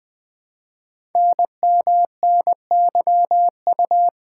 Morse code, heard as a keyed tone, about 20 wpm, 700 Hz. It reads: NMNYU
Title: Code Groups - 5 Letters - 20wpm